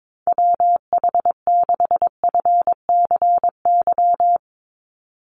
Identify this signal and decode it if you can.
Morse code, heard as a keyed tone, about 22 wpm, 700 Hz. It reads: WH6FCY
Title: USA Call Signs 22wpm